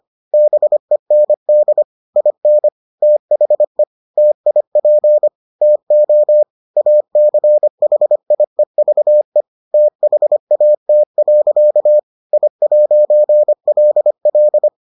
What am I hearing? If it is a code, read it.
Morse code, heard as a keyed tone, about 25 wpm, 600 Hz. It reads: BEND IN THE TIP TO ACHIEVE THAT. I'LL